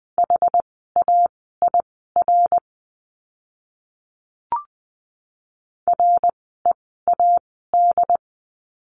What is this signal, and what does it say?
Morse code, heard as a keyed tone, about 20 wpm, spaced as for 15 wpm, 700 Hz. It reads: HAIR READ
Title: 101 Lesson 17f - Letter u - Words 20wpm